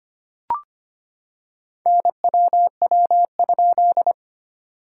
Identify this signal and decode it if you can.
Morse code, heard as a keyed tone, about 25 wpm, 690 Hz. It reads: NWW?
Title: SOTA QSO Practice 25wpm